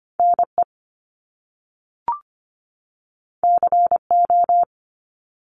Morse code, heard as a keyed tone, about 25 wpm, 700 Hz. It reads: NE CO
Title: Top Two Letter Combinations in Words 25wpm